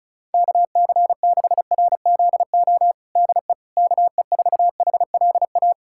Morse code, heard as a keyed tone, about 34 wpm, 700 Hz. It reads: KC6RZO DE KE4HLA
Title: USA Callsign DE Callsign - 6 Characters - 35wpm